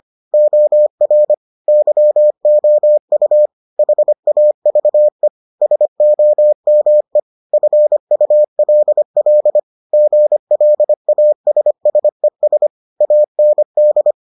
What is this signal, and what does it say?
Morse code, heard as a keyed tone, about 25 wpm, 600 Hz. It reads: OR YOU HAVE SOME FULL GLASSES AND